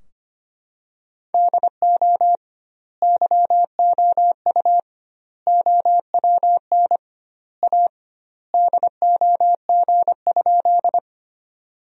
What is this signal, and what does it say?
Morse code, heard as a keyed tone, about 25 wpm, 700 Hz. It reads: DO YOU OWN A DOG?
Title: 2x Word-Spacing - Sentences from Top 400 Words - 25wpm